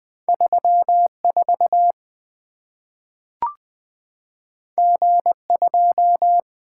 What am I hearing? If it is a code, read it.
Morse code, heard as a keyed tone, about 20 wpm, 700 Hz. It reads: 34 G2
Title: Two Letter-Number Combinations - 20wpm